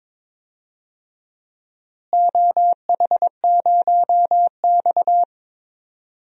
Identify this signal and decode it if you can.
Morse code, heard as a keyed tone, about 21 wpm, 700 Hz. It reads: OH0X
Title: DXCC-100 Call Signs - 4 Characters - Repeated 3x - 22wpm